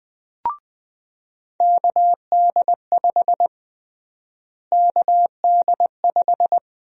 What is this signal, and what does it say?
Morse code, heard as a keyed tone, about 20 wpm, 700 Hz. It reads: KD5 KD5
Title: USA Call Sign Prefixes - Repeated 2x - 20wpm